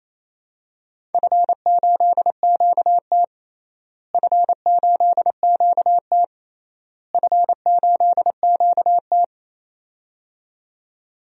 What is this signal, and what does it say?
Morse code, heard as a keyed tone, about 28 wpm, 700 Hz. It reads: F8QT F8QT F8QT
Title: DXCC-100 Call Signs - 4 Characters - Repeated 3x - 28wpm